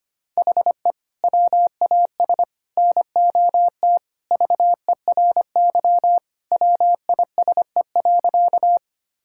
Morse code, heard as a keyed tone, about 25 wpm, 700 Hz. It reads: HE WAS NOT VERY WISE.
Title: Sentences from Top 1600 Words 25wpm